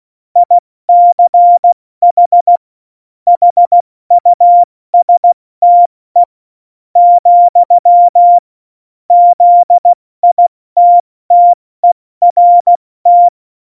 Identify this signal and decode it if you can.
Morse code, heard as a keyed tone, about 16 wpm, spaced as for 13 wpm, 700 Hz. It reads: ICH HUSTE, ZITTERT
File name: Morsekurs / Witze / Witz 18. Dezember, 70 F80